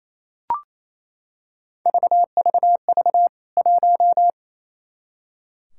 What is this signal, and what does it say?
Morse code, heard as a keyed tone, about 27 wpm, 700 Hz. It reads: VVV 1
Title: K1USN SST QSO Practice 28wpm